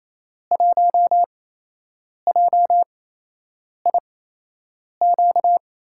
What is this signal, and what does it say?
Morse code, heard as a keyed tone, about 28 wpm, spaced as for 8 wpm, 700 Hz. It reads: 1JIQ
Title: USA Call Signs - 6 Characters - 8x Character Spacing - 28wpm